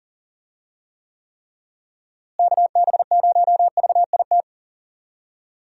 Morse code, heard as a keyed tone, about 40 wpm, 700 Hz. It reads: KB0VIT
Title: USA Call Signs - 6 Characters - Repeated 3x - 40wpm